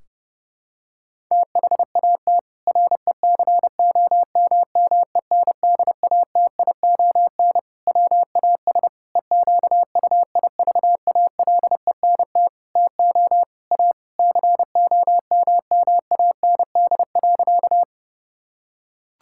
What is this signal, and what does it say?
Morse code, heard as a keyed tone, about 32 wpm, 700 Hz. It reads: THAT RECOMMENDATION WAS EQUIVALENT TO A COMMAND.
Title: Sentences from Top 1600 Words 30wpm